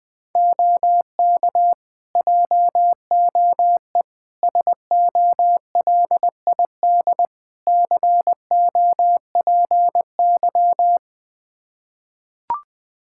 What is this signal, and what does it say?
Morse code, heard as a keyed tone, about 21 wpm, 700 Hz. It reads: OK JOE SOLID COPY
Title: K1USN SST QSO Practice 20wpm